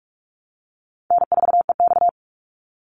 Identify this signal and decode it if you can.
Morse code, heard as a keyed tone, about 44 wpm, 700 Hz. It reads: N4EX
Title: NA SOTA Activator Callsigns 45wpm